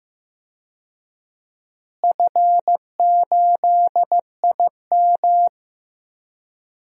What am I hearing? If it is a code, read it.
Morse code, heard as a keyed tone, about 15 wpm, 700 Hz. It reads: F8IM
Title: DXCC-100 Call Signs - 4 Characters - Repeated 3x - 15wpm